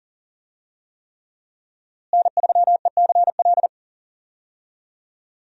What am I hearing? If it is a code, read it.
Morse code, heard as a keyed tone, about 40 wpm, 700 Hz. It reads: N3ECL
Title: USA Call Signs - 5 Characters - Repeated 3x - 40wpm